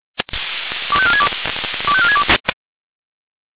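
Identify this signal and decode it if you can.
Five-tone selective call (ZVEI), heard as quick runs of five tones, tones of about 70 ms. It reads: 25462 25462
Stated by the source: ZVEI five-tone call received by SDR.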